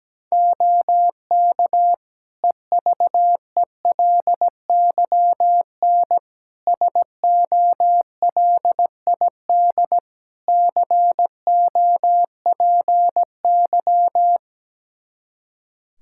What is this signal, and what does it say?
Morse code, heard as a keyed tone, about 17 wpm, 700 Hz. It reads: OK EVELYN SOLID COPY
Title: Practice QSO - Exchange 3 - More Info - 17wpm